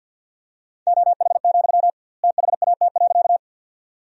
Morse code, heard as a keyed tone, about 46 wpm, 700 Hz. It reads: OH, THAT.
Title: 2x Word-Spacing - Sentences from Top 500 Words - 50wpm